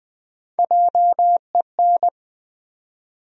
Morse code, heard as a keyed tone, about 20 wpm, 700 Hz. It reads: JEN
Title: USA Common Names - Max 3 Letters - 20wpm